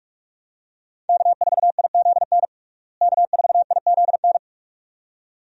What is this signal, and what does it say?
Morse code, heard as a keyed tone, about 42 wpm, 700 Hz. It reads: K4IZN K4IZN
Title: USA Call Signs - 5 Characters - Repeated 2x - 45wpm